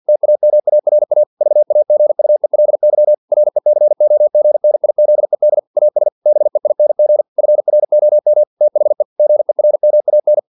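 Morse code, heard as a keyed tone, about 50 wpm, 600 Hz. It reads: TAMARA VAGUELY RECOGNIZED AS BEING FROM THE GERMAN
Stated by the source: Morse practice text from an English-language novel at 49 wpm.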